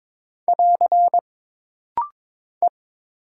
Morse code, heard as a keyed tone, about 22 wpm, 700 Hz. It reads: <AR> E
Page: This rapid-fire timing is geared to push your ICR (Instant Character Recognition) skills.